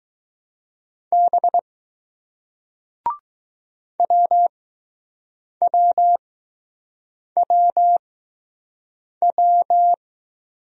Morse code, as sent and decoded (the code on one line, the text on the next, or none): -... / .-- .-- .-- .--
B WWWW